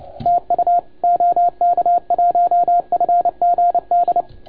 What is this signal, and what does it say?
Morse code, heard as a keyed tone, about 29 wpm, 690 Hz, with radio noise. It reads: TU OK1FGD
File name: Archiv / CUC287 / OK1FGD CUC 287